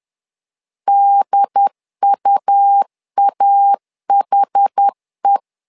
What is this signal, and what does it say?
Morse code, heard as a keyed tone, about 10 wpm, 800 Hz. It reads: DUAHE
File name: LEZIONI / 2 UHDN / 2 UHD 10wpm 1826